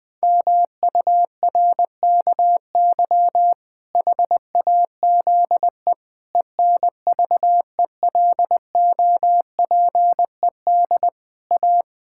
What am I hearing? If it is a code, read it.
Morse code, heard as a keyed tone, about 20 wpm, 700 Hz. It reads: MURKY HAZE ENVELOPED A
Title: Pangram Challenge v2 - 20wpm